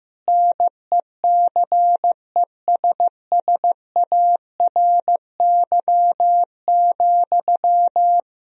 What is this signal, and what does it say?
Morse code, heard as a keyed tone, about 15 wpm, 700 Hz. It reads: NECESSARY,
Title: Sentences from Top 1000 Words - Review Part 4 - 15wpm